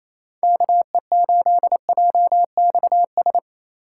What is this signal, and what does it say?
Morse code, heard as a keyed tone, about 28 wpm, 700 Hz. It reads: KE8JXS
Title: USA Call Signs - 6 Characters - 8x Character Spacing - 28wpm